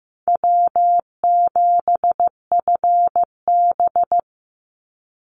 Morse code, heard as a keyed tone, about 15 wpm, 700 Hz. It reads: W7FB
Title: Easy Call Signs - 15wpm